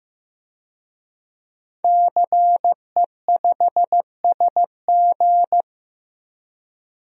Morse code, heard as a keyed tone, about 15 wpm, 700 Hz. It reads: CE5SG
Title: DXCC-100 Call Signs - 5 Characters - 15wpm